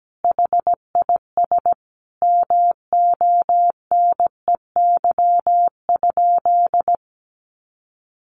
Morse code, heard as a keyed tone, about 17 wpm, 700 Hz. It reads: HIS MONEY?
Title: Sentences from Top 400 Words 17wpm